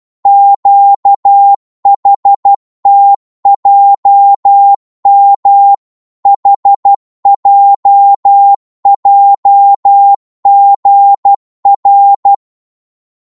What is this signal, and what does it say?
Morse code, heard as a keyed tone, about 12 wpm, 800 Hz. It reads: QHTJM HJJGR